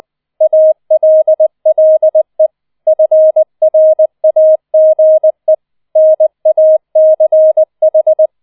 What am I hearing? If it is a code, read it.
Morse code, heard as a keyed tone, about 20 wpm, 630 Hz. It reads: ALLE FRAGE NACH